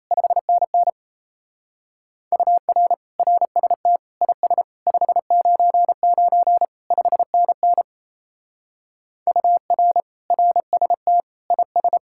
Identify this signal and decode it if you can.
Morse code, heard as a keyed tone, about 34 wpm, 700 Hz. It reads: 5NN UR RST IS 599 5NN UR RST IS
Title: Speed-Racing Common QSO Phrases - 28wpm